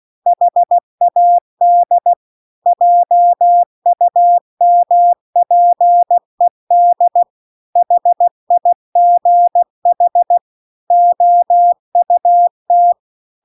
Morse code, heard as a keyed tone, about 16 wpm, 700 Hz. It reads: HAD JUMPED HIGH OUT